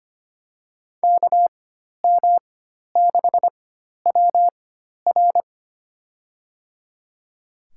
Morse code, heard as a keyed tone, about 25 wpm, spaced as for 12 wpm, 700 Hz. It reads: KM6WR
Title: USA Call Signs - 5 Characters - 4x Character Spacing - 25wpm